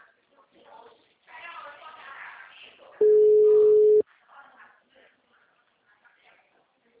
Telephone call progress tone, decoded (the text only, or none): ringback tone